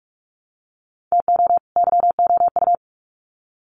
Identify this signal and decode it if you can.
Morse code, heard as a keyed tone, about 44 wpm, 700 Hz. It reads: TO YOU
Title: Sets of 2 Words 45wpm